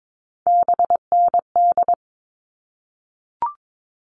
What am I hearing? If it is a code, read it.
Morse code, heard as a keyed tone, about 22 wpm, 700 Hz. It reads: BND
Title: All Cut Numbers - 3-Digit 22wpm